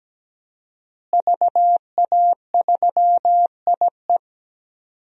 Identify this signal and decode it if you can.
Morse code, heard as a keyed tone, about 17 wpm, 700 Hz. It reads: VA3IE
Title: DXCC-100 Call Signs - 5 Characters - Repeated 2x - 17wpm